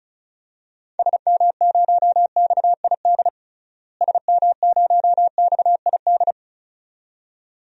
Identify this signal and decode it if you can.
Morse code, heard as a keyed tone, about 36 wpm, 700 Hz. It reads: SM0XID SM0XID
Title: DXCC-100 Call Signs - 6 Characters - Repeated 2x - 35wpm